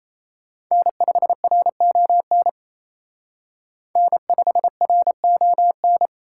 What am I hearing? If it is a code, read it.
Morse code, heard as a keyed tone, about 32 wpm, 700 Hz. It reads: N5RON N5RON
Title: Speed Racing - USA 1x3 Call Signs 28wpm